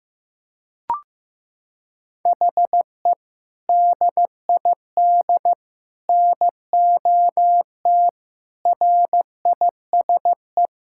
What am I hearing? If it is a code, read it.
Morse code, heard as a keyed tone, about 15 wpm, 700 Hz. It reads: HE DID NOT RISE
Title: Sentences from Top 1000 Words - Review Part 6 - 15wpm